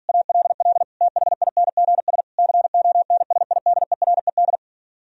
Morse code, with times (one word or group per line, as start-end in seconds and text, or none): 0.1-0.8 ALL
1.0-2.2 THINGS
2.4-4.6 CONSIDERED